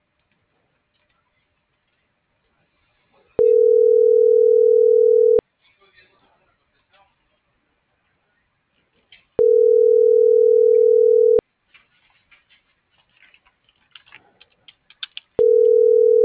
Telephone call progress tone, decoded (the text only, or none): ringback tone